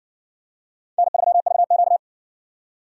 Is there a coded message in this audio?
Yes: Morse code from 1.0 to 2.0 s, heard as a keyed tone, about 54 wpm, 700 Hz. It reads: N3VQ